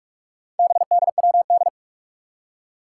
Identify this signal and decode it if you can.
Morse code, heard as a keyed tone, about 44 wpm, 700 Hz. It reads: BDWD
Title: Four Challenging Letter-Number Combinations 45wpm